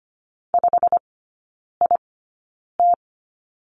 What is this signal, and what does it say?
Morse code, heard as a keyed tone, about 25 wpm, spaced as for 9 wpm, 700 Hz. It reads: 5IT
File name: course / 25:10wpm / 093 Lesson 16e - Letter c - Three Characters 25wpm